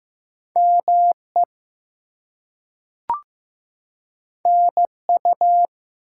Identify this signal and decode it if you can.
Morse code, heard as a keyed tone, about 15 wpm, 700 Hz. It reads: ME NU
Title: US States and Canadian Provinces 15wpm